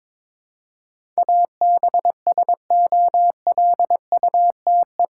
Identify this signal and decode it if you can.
Morse code, heard as a keyed tone, about 22 wpm, 700 Hz. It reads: ABSOLUTE
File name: practice / 22wpm / Eight Letter Words - 22wpm